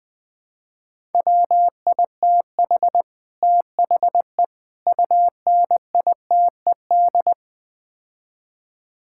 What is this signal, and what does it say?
Morse code, heard as a keyed tone, about 20 wpm, 700 Hz. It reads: WITH THE UNITED
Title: Build A Phrase - 4-Word Phrases - 20wpm